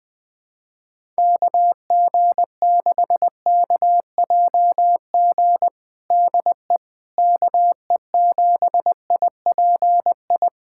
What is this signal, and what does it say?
Morse code, heard as a keyed tone, about 20 wpm, 700 Hz. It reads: KG6KJG DE KE7IPI
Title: USA Callsign DE Callsign - 6 Characters - 20wpm